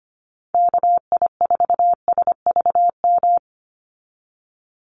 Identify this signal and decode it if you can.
Morse code, heard as a keyed tone, about 25 wpm, 700 Hz. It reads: KI4SVM
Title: NA SOTA Activator Callsigns 25wpm